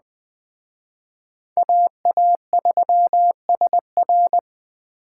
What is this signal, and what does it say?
Morse code, heard as a keyed tone, about 20 wpm, 700 Hz. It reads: AA3SR
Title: Speed Racing - USA 2x2 Call Signs 17wpm